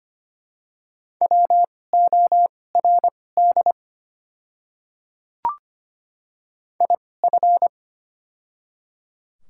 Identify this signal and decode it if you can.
Morse code, heard as a keyed tone, about 25 wpm, spaced as for 18 wpm, 700 Hz. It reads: WORD IF
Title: 2x Character Spacing - Top 100 Words - 25wpm